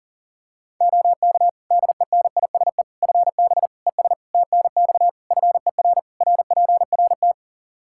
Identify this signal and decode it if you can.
Morse code, heard as a keyed tone, about 40 wpm, 700 Hz. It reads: OK DENISE FB ES TNX FER RPRT
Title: Practice QSO - Exchange 2 - Info - 40wpm